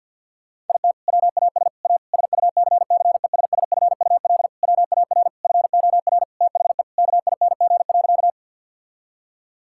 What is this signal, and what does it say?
Morse code, heard as a keyed tone, about 51 wpm, 700 Hz. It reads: IT WAS A SUCCESSFUL WAR FOR THE KING.
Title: Sentences from Top 1000 Words - Review Part 6 - 50wpm